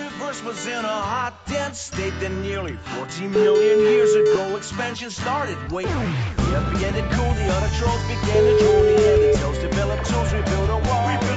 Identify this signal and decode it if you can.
Ringback tone (Europe, 425 Hz), with speech.